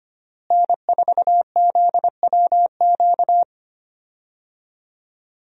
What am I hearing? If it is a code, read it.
Morse code, heard as a keyed tone, about 25 wpm, 700 Hz. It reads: N4ZWQ
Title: USA Call Signs - 5 Characters - Repeated 3x - 25wpm